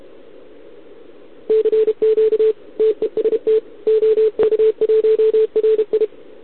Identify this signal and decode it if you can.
Morse code, heard as a keyed tone, about 28 wpm, 440 Hz. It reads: CQ TEST OK1RI
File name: ARCHIV / OKDX2020 / OK1RIb